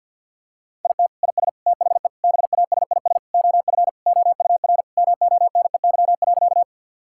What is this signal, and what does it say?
Morse code, heard as a keyed tone, about 50 wpm, 700 Hz. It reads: IT IS THE BASIS OF OUR MONEY.
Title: Sentences from Top 1000 Words - Review Part 5 - 50wpm